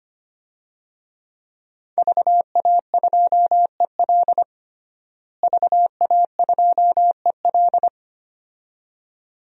Morse code, heard as a keyed tone, about 25 wpm, 700 Hz. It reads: VA2EL VA2EL
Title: DXCC-100 Call Signs - 5 Characters - Repeated 2x - 25wpm